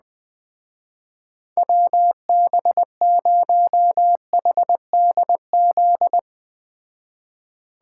Morse code, heard as a keyed tone, about 20 wpm, 700 Hz. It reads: WB0HDZ
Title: Speed Racing - USA 2x3 Call Signs 17wpm